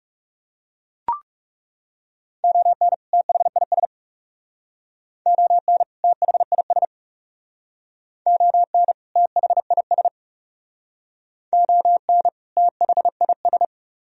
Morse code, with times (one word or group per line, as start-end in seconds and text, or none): 2.4-2.9 ON
3.1-3.9 THIS
5.3-5.8 ON
6.0-6.8 THIS
8.3-8.9 ON
9.2-10.1 THIS
11.5-12.3 ON
12.6-13.7 THIS